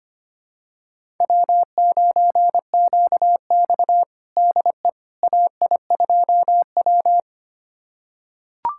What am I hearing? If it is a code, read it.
Morse code, heard as a keyed tone, about 25 wpm, 700 Hz. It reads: W9QX DE AI2W
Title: USA Callsign DE Callsign - 4 Characters - 25wpm